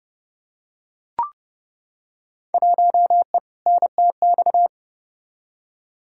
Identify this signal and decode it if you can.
Morse code, heard as a keyed tone, about 30 wpm, 700 Hz. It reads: 1E NTX
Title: Field Day Exchanges - 30wpm